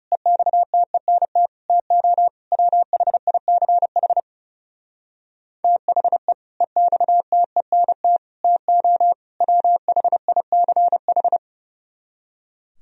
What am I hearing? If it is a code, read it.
Morse code, heard as a keyed tone, about 32 wpm, 700 Hz. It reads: EXTENT TO WHICH THE EXTENT TO WHICH
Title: Speed-Racing - Sets of 4 Words 30wpm